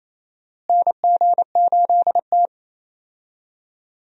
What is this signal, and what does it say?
Morse code, heard as a keyed tone, about 27 wpm, 700 Hz. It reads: NG8T
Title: USA Call Signs - 4 Characters - Repeated 2x - 28wpm